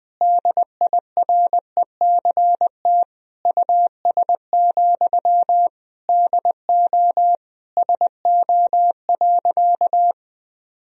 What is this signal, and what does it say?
Morse code, heard as a keyed tone, about 20 wpm, 700 Hz. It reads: DIRECT US, DO SO.
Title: Sentences from Top 1000 Words - Review Part 2 - 20wpm